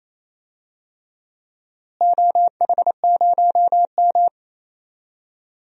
Morse code, heard as a keyed tone, about 27 wpm, 700 Hz. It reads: OH0M
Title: DXCC-100 Call Signs - 4 Characters - Repeated 3x - 28wpm